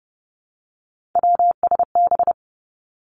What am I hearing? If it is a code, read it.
Morse code, heard as a keyed tone, about 32 wpm, 700 Hz. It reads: WSB
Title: Three Challenging Letter-Number Combinations 30wpm